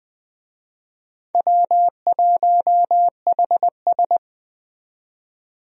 Morse code, heard as a keyed tone, about 20 wpm, 700 Hz. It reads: W1HS
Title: USA Call Signs - 4 Characters - 20wpm